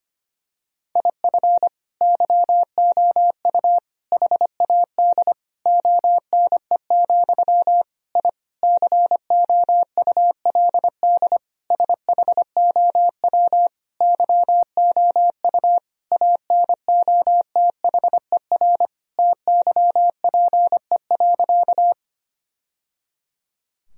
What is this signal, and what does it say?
Morse code, heard as a keyed tone, about 25 wpm, 700 Hz. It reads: IF YOU HAD ONE, I COULD SHOW YOU ANOTHER TYPE.
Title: Sentences from Top 1000 Words - Review Part 3 - 25wpm